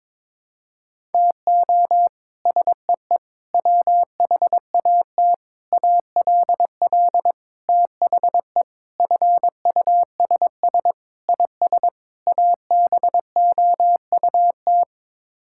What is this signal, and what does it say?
Morse code, heard as a keyed tone, about 21 wpm, 700 Hz. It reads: TO SEE WHAT ALL THE FUSS IS ABOUT
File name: practice / 22wpm / Sets of 8 Words - 22wpm